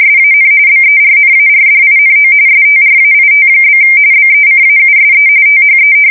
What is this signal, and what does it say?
RTTY, about 45.45 baud. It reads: SAN BERNARDINO STRAIT? BECAUSE THE
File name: fsk170 / battleshipnj